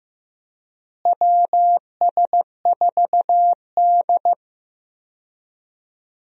Morse code, heard as a keyed tone, about 15 wpm, 700 Hz. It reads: WS4D
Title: USA Call Signs - 4 Characters - Repeated 2x - 15wpm